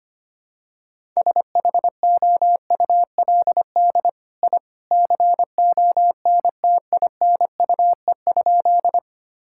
Morse code, heard as a keyed tone, about 25 wpm, 700 Hz. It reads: SHOULD I CONTINUE?
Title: Sentences from Top 1000 Words - Review Part 6 - 25wpm